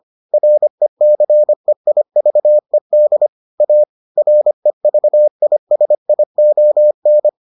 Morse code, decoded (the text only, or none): RECEIVED A REVISION